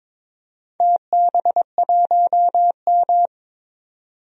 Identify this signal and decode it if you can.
Morse code, heard as a keyed tone, about 22 wpm, 700 Hz. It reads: TB1M
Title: DXCC-100 Call Signs - 4 Characters - 22wpm